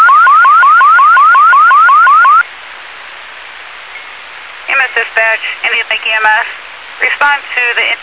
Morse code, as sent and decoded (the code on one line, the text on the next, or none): none